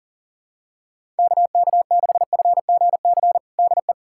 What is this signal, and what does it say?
Morse code, heard as a keyed tone, about 40 wpm, 700 Hz. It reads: KK6FGC DE
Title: USA Callsign DE Callsign 40wpm